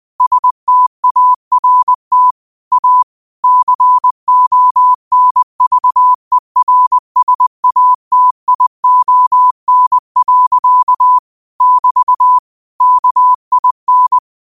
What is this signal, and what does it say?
Morse code, heard as a keyed tone, about 20 wpm, 1000 Hz. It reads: START A CONVERSATION. <BT> KIN
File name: qotdfiles / 20wpm